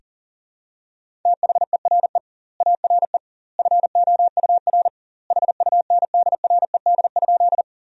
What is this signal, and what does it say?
Morse code, heard as a keyed tone, about 40 wpm, 700 Hz. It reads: THERE ARE FOUR HUNDRED?
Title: 2x Word-Spacing - Sentences from Top 400 Words - 40wpm